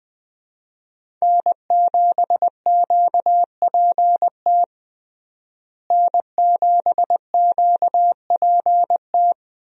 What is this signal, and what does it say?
Morse code, heard as a keyed tone, about 21 wpm, 700 Hz. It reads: N7QPT N7QPT
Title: USA Call Signs - 5 Characters - Repeated 3x - 20wpm